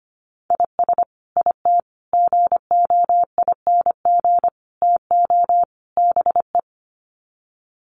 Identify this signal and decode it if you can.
Morse code, heard as a keyed tone, about 25 wpm, 700 Hz. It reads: IS IT GOING TO BE
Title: Sets of 5 Words Encore2 25wpm